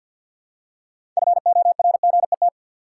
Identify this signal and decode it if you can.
Morse code, heard as a keyed tone, about 53 wpm, 700 Hz. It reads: FORGET
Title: Six Letter Words - 50wpm